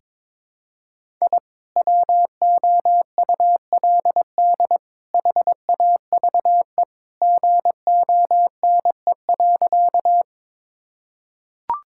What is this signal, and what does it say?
Morse code, heard as a keyed tone, about 22 wpm, 700 Hz. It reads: I WOULD HAVE GONE.
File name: practice / 22wpm / Sentences with 3-Word Phrases - 22wpm